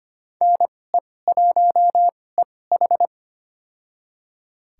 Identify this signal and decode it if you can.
Morse code, heard as a keyed tone, about 25 wpm, spaced as for 18 wpm, 700 Hz. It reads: NE1EH
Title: USA Call Signs - 2x Character Spacing - 25wpm